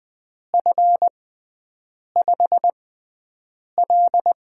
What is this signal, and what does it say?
Morse code, heard as a keyed tone, about 20 wpm, spaced as for 7 wpm, 700 Hz. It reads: F5L
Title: USA Call Signs - 4 Characters - 6x Character Spacing - 20wpm